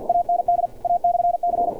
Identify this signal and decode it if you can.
Morse code, heard as a keyed tone, about 50 wpm, 700 Hz. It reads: AND A/C
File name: archived / CRSnet